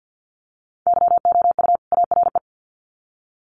Morse code, heard as a keyed tone, about 50 wpm, 700 Hz. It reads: YOU ARE
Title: Sets of 2 Words 50wpm